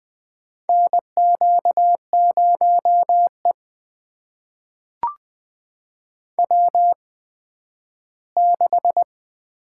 Morse code, heard as a keyed tone, about 20 wpm, spaced as for 12 wpm, 700 Hz. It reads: NQ0E W6
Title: USA Call Signs - 4 Characters - 8x Character Spacing - 20wpm